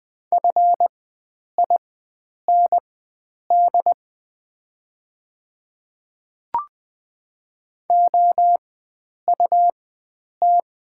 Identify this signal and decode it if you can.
Morse code, heard as a keyed tone, about 20 wpm, spaced as for 9 wpm, 690 Hz. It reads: FIND OUT
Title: 4x Character Spacing - Top 100 Words - 20wpm